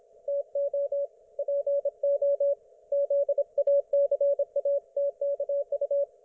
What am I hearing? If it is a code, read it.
Morse code, heard as a keyed tone, about 27 wpm, 560 Hz, with radio noise. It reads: TO PO ZACATKU